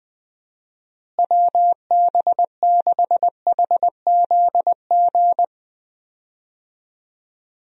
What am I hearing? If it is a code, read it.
Morse code, heard as a keyed tone, about 21 wpm, 700 Hz. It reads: WB6HZG
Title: USA Call Signs - 6 Characters - Repeated 3x - 20wpm